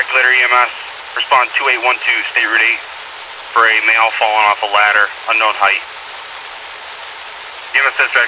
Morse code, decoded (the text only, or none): none